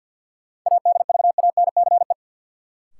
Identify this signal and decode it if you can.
Morse code, heard as a keyed tone, about 51 wpm, 700 Hz. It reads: ADVANCE
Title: Sentences from Top 1400 Words - New Word Drill Alt - 50wpm